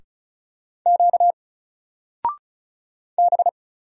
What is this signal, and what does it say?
Morse code, heard as a keyed tone, about 36 wpm, 700 Hz. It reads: Q B
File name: Single-Letter-Number-35wpm